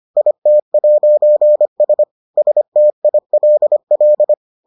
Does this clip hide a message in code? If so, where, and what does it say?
Yes: Morse code from 0.2 to 4.3 s, heard as a keyed tone, about 25 wpm, 600 Hz. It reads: IT'S STILL